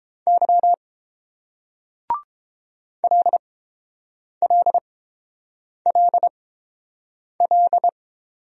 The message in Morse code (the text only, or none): Y LLLL